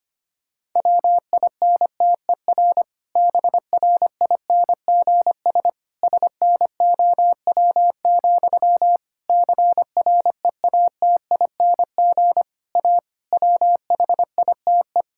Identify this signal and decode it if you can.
Morse code, heard as a keyed tone, about 25 wpm, 700 Hz. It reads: WINTER BRINGS SNOW, CREATING A WHITE
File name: practice / 25wpm / Difficult Sentences with Hints - 25wpm